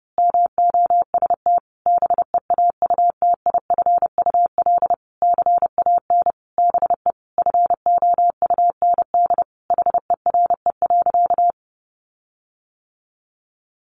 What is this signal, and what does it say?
Morse code, heard as a keyed tone, about 30 wpm, 700 Hz. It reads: MOST BEAUTIFUL CAN BE FOUND HERE.
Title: Sentences from Top 1500 Words 30wpm